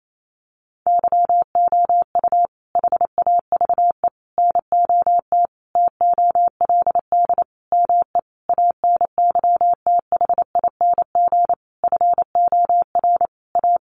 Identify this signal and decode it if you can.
Morse code, heard as a keyed tone, about 28 wpm, 700 Hz. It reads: YOU HAVE NOT TOLD ME ANYTHING FOR A
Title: Sentences from Top 1000 Words 28wpm